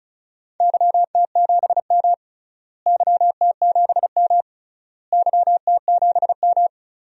Morse code, heard as a keyed tone, about 36 wpm, 700 Hz. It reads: YT7M YT7M YT7M
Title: DXCC-100 Call Signs - 4 Characters - Repeated 3x - 35wpm